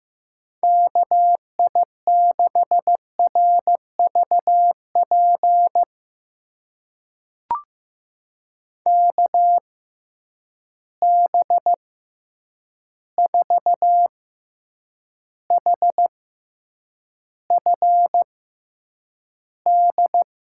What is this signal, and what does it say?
Morse code, heard as a keyed tone, about 15 wpm, spaced as for 8 wpm, 700 Hz. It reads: KI6RVP KB4HFD
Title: USA Call Signs - 6 Characters - 6x Character Spacing - 15wpm